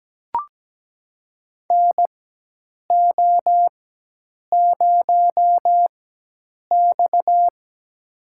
Morse code, heard as a keyed tone, about 17 wpm, spaced as for 8 wpm, 700 Hz. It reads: NO0X
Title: USA Call Signs - 4 Characters - 4x Character Spacing - 17wpm